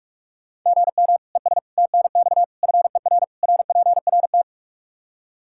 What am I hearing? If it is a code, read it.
Morse code, heard as a keyed tone, about 46 wpm, 700 Hz. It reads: GM ES TNX FER RPRT
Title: Practice QSO - Exchange 1 - Beginning - 45wpm